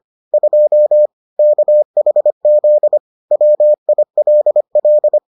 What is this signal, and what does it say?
Morse code, heard as a keyed tone, about 25 wpm, 600 Hz. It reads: 2 KHZ WILL